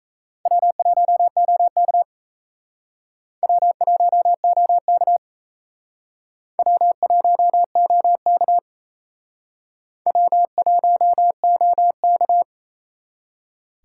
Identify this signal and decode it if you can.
Morse code, heard as a keyed tone, about 34 wpm, 700 Hz. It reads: W1OK W1OK W1OK W1OK
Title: Speed Racing - USA 1x2 Call Signs 28wpm